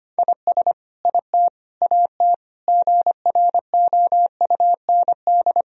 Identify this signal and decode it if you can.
Morse code, heard as a keyed tone, about 25 wpm, 700 Hz. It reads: IS IT AT GROUND
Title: Sentences from Top 1000 Words - Review Part 4 - 25wpm